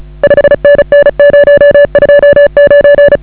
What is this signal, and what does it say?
Morse code, heard as a keyed tone, about 35 wpm, 560 Hz. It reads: 5NN 029